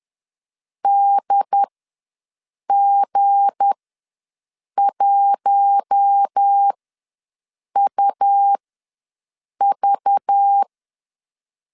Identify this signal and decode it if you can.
Morse code, heard as a keyed tone, about 10 wpm, spaced as for 6 wpm, 800 Hz. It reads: DG1UV
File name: LEZIONI / 9 28 ws 10wpm 4124